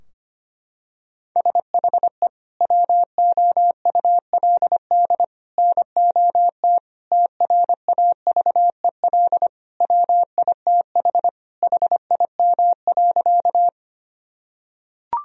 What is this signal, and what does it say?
Morse code, heard as a keyed tone, about 25 wpm, 700 Hz. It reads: SHE WOULD NOT TRAVEL WITH HIM.
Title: Sentences from Top 1000 Words - Review Part 2 - 25wpm